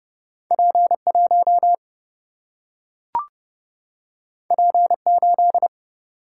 Morse code, heard as a keyed tone, about 30 wpm, 700 Hz. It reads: P 1 P 8
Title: Single-Letter Single-Number 30wpm